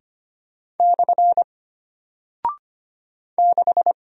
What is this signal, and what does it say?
Morse code, heard as a keyed tone, about 25 wpm, 700 Hz. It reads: / 6
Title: Single Character (Letter:Number:Punctuation:Prosign) Alt 25wpm